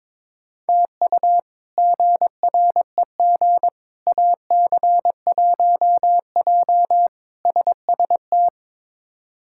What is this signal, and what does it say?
Morse code, heard as a keyed tone, about 22 wpm, 700 Hz. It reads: TU GREG AC1J SST
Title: K1USN SST QSO Practice - 22wpm